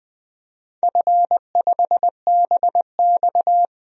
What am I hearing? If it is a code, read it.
Morse code, heard as a keyed tone, about 20 wpm, 700 Hz. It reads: F5BX
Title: DXCC-100 Call Signs - 4 Characters - 20wpm